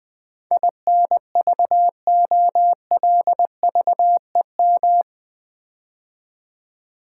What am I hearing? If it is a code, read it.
Morse code, heard as a keyed tone, about 20 wpm, 700 Hz. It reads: INVOLVEM
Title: Build A Word - 4 Letters At A Time - 11-Letter Words - 20wpm